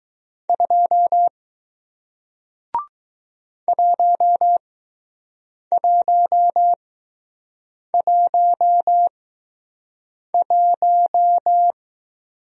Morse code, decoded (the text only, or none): 2 1111